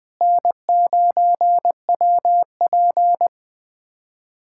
Morse code, heard as a keyed tone, about 21 wpm, 700 Hz. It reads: N9WP
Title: USA Callsign DE Callsign - 4 Characters - 20wpm